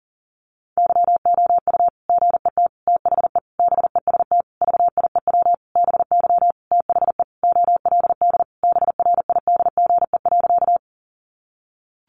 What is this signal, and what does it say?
Morse code, heard as a keyed tone, about 40 wpm, 700 Hz. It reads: YOU GET THE BEST VIEW BY THE OLD BRIDGE.
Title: Sentences from Top 1200 Words 40wpm